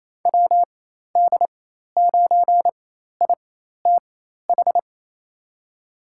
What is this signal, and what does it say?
Morse code, heard as a keyed tone, about 28 wpm, spaced as for 13 wpm, 700 Hz. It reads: WD9ITH
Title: USA Call Signs - 6 Characters - 4x Character Spacing - 28wpm